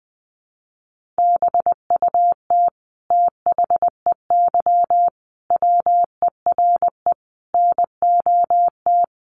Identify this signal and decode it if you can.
Morse code, heard as a keyed tone, about 20 wpm, 700 Hz. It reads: BUT THEY WERE NOT
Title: Sentences from Top 1400 Words 20wpm